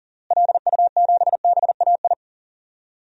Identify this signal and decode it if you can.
Morse code, heard as a keyed tone, about 37 wpm, 700 Hz. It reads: LU7BAI
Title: DXCC-100 Call Signs - 6 Characters - Repeated 2x - 40wpm